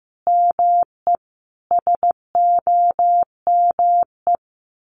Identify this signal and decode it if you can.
Morse code, heard as a keyed tone, about 15 wpm, 700 Hz. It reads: ME SOME
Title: Sentences from Top 1500 Words 15wpm